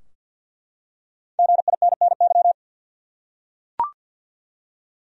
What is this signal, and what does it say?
Morse code, heard as a keyed tone, about 48 wpm, 700 Hz. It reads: GINNY
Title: USA Common Names - Max 5 Letters - 50wpm